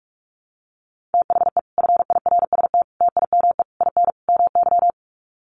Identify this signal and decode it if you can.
Morse code, heard as a keyed tone, about 44 wpm, 700 Hz. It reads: THE FIRST TIME IN MY
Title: Sets of 5 Words Encore 45wpm